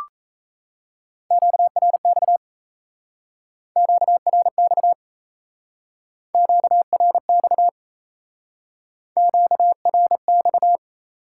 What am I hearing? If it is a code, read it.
Morse code, heard as a keyed tone, about 35 wpm, 700 Hz. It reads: QRX QRX QRX QRX
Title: Speed-Racing Top 180 QSO Elements 28wpm